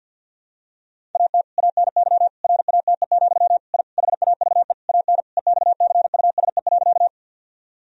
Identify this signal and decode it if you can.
Morse code, heard as a keyed tone, about 52 wpm, 700 Hz. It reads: AT ANY RATE, I HAVE AN EXCUSE.
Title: Sentences from Top 1000 Words - Review Part 4 - 50wpm